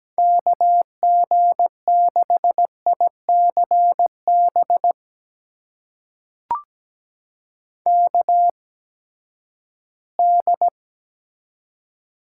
Morse code, heard as a keyed tone, about 17 wpm, spaced as for 12 wpm, 700 Hz. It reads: KG6ICB KD
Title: USA Call Signs - 6 Characters - 8x Character Spacing - 17wpm